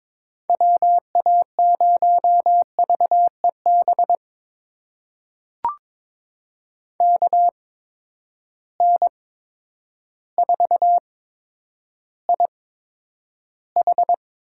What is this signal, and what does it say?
Morse code, heard as a keyed tone, about 22 wpm, spaced as for 10 wpm, 700 Hz. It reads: WA0VEB KN4IH
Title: USA Call Signs - 6 Characters - 8x Character Spacing - 22wpm